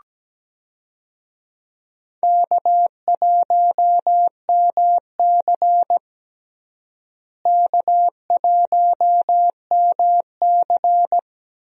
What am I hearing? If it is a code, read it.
Morse code, heard as a keyed tone, about 17 wpm, 700 Hz. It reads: K1MC K1MC
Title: USA Call Signs - 4 Characters - Repeated 3x - 17wpm